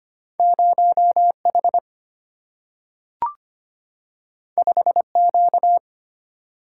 Morse code, heard as a keyed tone, about 25 wpm, 700 Hz. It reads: 0 H 5 Q
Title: Single-Letter Single-Number 25wpm